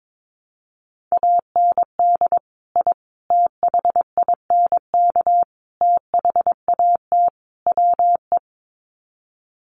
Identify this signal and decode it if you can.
Morse code, heard as a keyed tone, about 22 wpm, 700 Hz. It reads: AND I THINK THAT WE
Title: Sets of 5 Words Encore2 22wpm